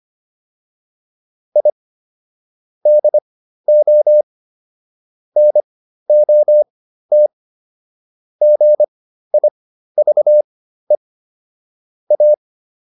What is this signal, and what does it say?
Morse code, heard as a keyed tone, about 25 wpm, spaced as for 13 wpm, 600 Hz. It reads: I DO NOT GIVE A